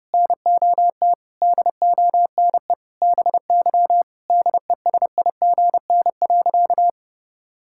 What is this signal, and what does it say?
Morse code, heard as a keyed tone, about 30 wpm, 700 Hz. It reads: NOT DONE BY DESIGN.
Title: Sentences from Top 1000 Words - Review Part 5 - 30wpm